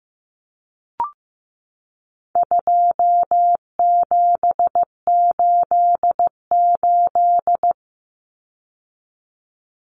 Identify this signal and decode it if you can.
Morse code, heard as a keyed tone, about 15 wpm, 700 Hz. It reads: 2788
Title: Common Cut Numbers - 4-Digit 15wpm